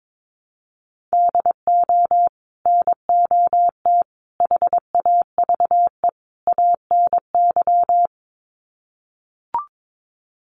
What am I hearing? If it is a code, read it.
Morse code, heard as a keyed tone, about 22 wpm, 700 Hz. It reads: DO NOT HAVE ANY
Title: Sets of 4 Words 22wpm